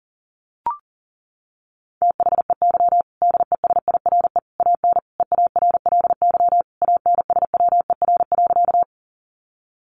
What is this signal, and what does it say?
Morse code, heard as a keyed tone, about 40 wpm, 700 Hz. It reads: THEY DESIRE AN EARLY ANSWER.
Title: Sentences from Top 1100 Words 40wpm